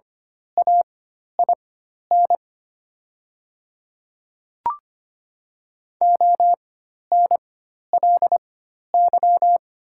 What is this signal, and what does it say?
Morse code, heard as a keyed tone, about 25 wpm, spaced as for 12 wpm, 700 Hz. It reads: AIN ONLY
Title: 4x Character Spacing - Top 200 Words - 25wpm